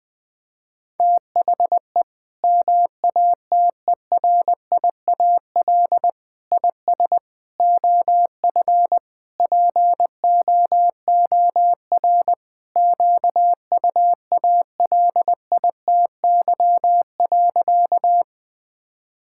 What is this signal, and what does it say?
Morse code, heard as a keyed tone, about 20 wpm, 700 Hz. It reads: THE MATERIAL IS OF POOR QUALITY.
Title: Sentences from Top 1000 Words - Review Part 5 - 20wpm